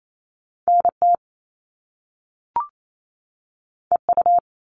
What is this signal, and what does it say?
Morse code, heard as a keyed tone, about 28 wpm, 700 Hz. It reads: NT EU
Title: Top Two Letter Combinations in Words 28wpm